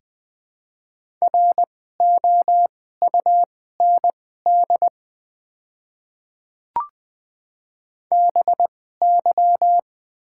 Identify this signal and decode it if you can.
Morse code, heard as a keyed tone, about 20 wpm, spaced as for 15 wpm, 690 Hz. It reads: ROUND BY
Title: 2x Character Spacing - Top 200 Words - 20wpm